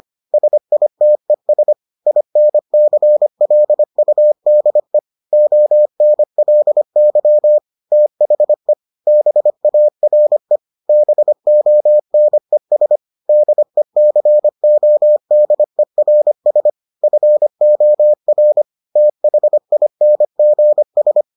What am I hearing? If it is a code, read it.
Morse code, heard as a keyed tone, about 25 wpm, 600 Hz. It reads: SITES INCLUDE ONLY THE BARE BONES DECODERS FOR THINGS